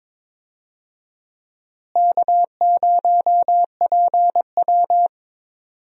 Morse code, heard as a keyed tone, about 22 wpm, 700 Hz. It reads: K0PW
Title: USA Call Signs - 4 Characters - Repeated 3x - 22wpm